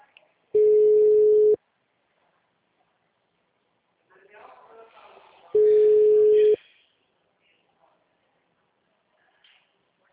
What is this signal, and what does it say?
Ringback tone (Europe, 425 Hz), with speech.
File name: out- -2007- - - .3625